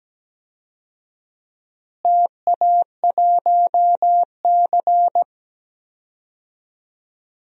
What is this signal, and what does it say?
Morse code, heard as a keyed tone, about 17 wpm, 700 Hz. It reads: TA1C
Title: DXCC-100 Call Signs - 4 Characters - Repeated 3x - 17wpm